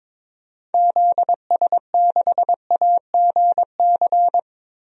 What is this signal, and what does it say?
Morse code, heard as a keyed tone, about 22 wpm, 700 Hz. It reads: ZS6AGC
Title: DXCC-100 Call Signs - 6 Characters - 22wpm